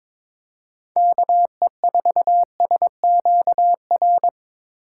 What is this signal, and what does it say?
Morse code, heard as a keyed tone, about 21 wpm, 700 Hz. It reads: KE4SQR
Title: USA Call Signs - 6 Characters - 8x Character Spacing - 22wpm